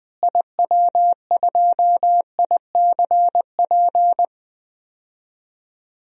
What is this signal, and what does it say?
Morse code, heard as a keyed tone, about 20 wpm, 700 Hz. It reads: IW2ICP
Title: DXCC-100 Call Signs - 6 Characters - Repeated 2x - 20wpm